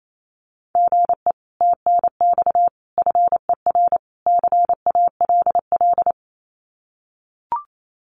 Morse code, heard as a keyed tone, about 28 wpm, 700 Hz. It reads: GE TNX FER CALL
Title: Common QSO Phrases Encore 28wpm